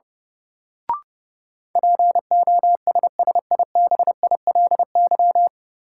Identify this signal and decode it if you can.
Morse code, heard as a keyed tone, about 31 wpm, 700 Hz. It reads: POSSIBILY
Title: Sentences from Top 1300 Words - New Word Drill Alt - 30wpm